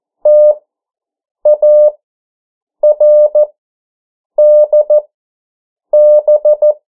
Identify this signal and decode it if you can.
Morse code, heard as a keyed tone, about 14 wpm, spaced as for 7 wpm, 600 Hz. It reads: TARDB
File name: morse mp3 / 09 urok 8